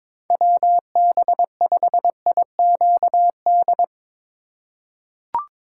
Morse code, heard as a keyed tone, about 21 wpm, 700 Hz. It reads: WB5IQD
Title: USA Call Signs - 6 Characters - 4x Character Spacing - 22wpm